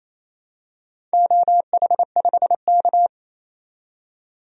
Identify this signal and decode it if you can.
Morse code, heard as a keyed tone, about 28 wpm, 700 Hz. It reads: OH5K
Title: DXCC-100 Call Signs - 4 Characters - 28wpm